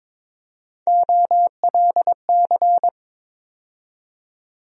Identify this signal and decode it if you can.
Morse code, heard as a keyed tone, about 21 wpm, 700 Hz. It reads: OLC
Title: USA Call Sign Postfixes - 3 Letters 22wpm